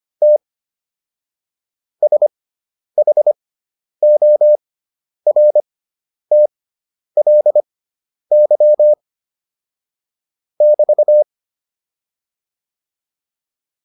Morse code, heard as a keyed tone, about 25 wpm, spaced as for 10 wpm, 600 Hz. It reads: T SHORTLY <BT>